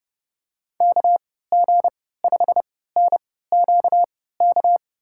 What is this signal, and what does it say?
Morse code, heard as a keyed tone, about 30 wpm, spaced as for 17 wpm, 700 Hz. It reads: KG5NQK
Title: USA Call Signs - 6 Characters - 3x Character Spacing - 30wpm